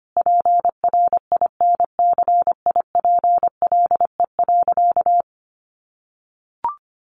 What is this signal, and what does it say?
Morse code, heard as a keyed tone, about 25 wpm, 700 Hz. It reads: PRINCIPLE.
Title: Sentences from Top 1400 Words 25wpm